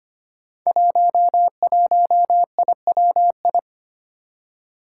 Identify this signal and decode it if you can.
Morse code, heard as a keyed tone, about 25 wpm, 700 Hz. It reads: 11IWI
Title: Five Challenging Letter-Number Combinations 25wpm